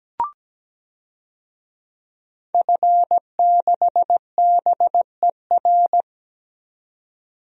Morse code, heard as a keyed tone, about 17 wpm, 700 Hz. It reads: F6BER
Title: DXCC-100 Call Signs - 5 Characters - 17wpm